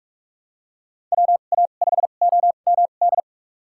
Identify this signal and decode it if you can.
Morse code, heard as a keyed tone, about 44 wpm, spaced as for 32 wpm, 700 Hz. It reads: WA5OMD
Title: USA Call Signs - 6 Characters - 2x Character Spacing - 45wpm